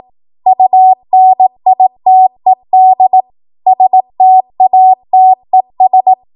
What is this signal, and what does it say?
Morse code, heard as a keyed tone, about 18 wpm, 750 Hz. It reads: UNITED STATES
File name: Morse / 18WPM